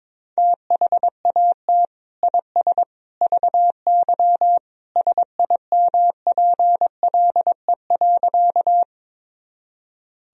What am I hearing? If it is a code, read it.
Morse code, heard as a keyed tone, about 21 wpm, 700 Hz. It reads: THAT IS VY SIMPLE.
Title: Sentences from Top 400 Words and Abbreviations 22wpm